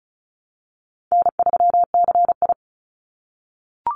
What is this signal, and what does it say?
Morse code, heard as a keyed tone, about 34 wpm, 700 Hz. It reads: N3CI
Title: Call Signs - 35wpm